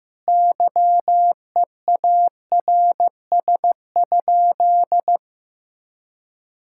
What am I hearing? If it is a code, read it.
Morse code, heard as a keyed tone, about 15 wpm, 700 Hz. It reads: YEARS?
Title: Sentences from Top 1000 Words - Review Part 6 - 15wpm